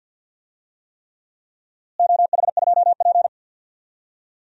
Morse code, heard as a keyed tone, about 50 wpm, 700 Hz. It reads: OH2P
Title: DXCC-100 Call Signs - 4 Characters - Repeated 3x - 50wpm